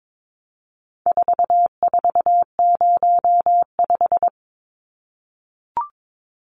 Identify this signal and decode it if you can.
Morse code, heard as a keyed tone, about 22 wpm, 700 Hz. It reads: 4405
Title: Numbers - 4-Digit 22wpm